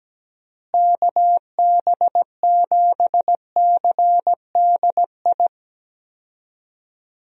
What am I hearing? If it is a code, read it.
Morse code, heard as a keyed tone, about 17 wpm, 700 Hz. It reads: KB7CDI
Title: USA Call Signs - 6 Characters - Repeated 3x - 17wpm